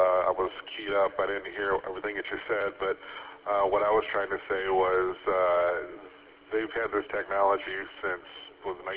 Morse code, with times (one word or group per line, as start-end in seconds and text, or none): none